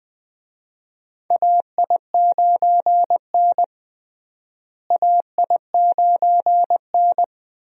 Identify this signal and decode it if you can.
Morse code, heard as a keyed tone, about 19 wpm, 700 Hz. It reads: AI9N AI9N
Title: USA Call Signs - 4 Characters - Repeated 3x - 20wpm